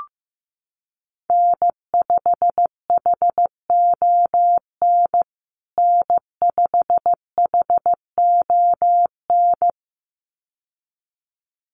Morse code, heard as a keyed tone, about 15 wpm, 700 Hz. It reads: N5HON N5HON
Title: USA Call Signs Repeated 2x 15wpm